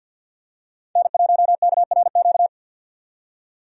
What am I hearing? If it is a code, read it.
Morse code, heard as a keyed tone, about 51 wpm, 700 Hz. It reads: N1KRQ
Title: USA Call Signs - 5 Characters - 50wpm